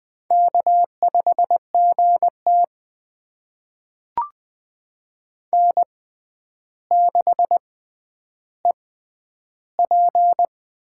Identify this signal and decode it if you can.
Morse code, heard as a keyed tone, about 20 wpm, spaced as for 10 wpm, 700 Hz. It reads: K5GT N6EP
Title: USA Call Signs - 4 Characters - 6x Character Spacing - 20wpm